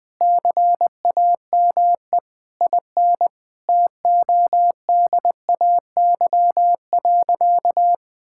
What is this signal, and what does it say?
Morse code, heard as a keyed tone, about 20 wpm, 700 Hz. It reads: CAME IN TODAY.
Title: Sentences with 4-Word Phrases - 20wpm